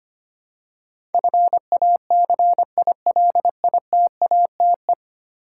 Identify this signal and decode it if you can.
Morse code, heard as a keyed tone, about 25 wpm, 700 Hz. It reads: FACILITATE
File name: practice / 25wpm / Ten Letter Words - 25wpm